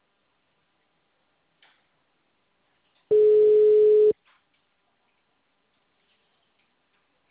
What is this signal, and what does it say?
Ringback tone (Europe, 425 Hz).
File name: out- -2007- - - .48066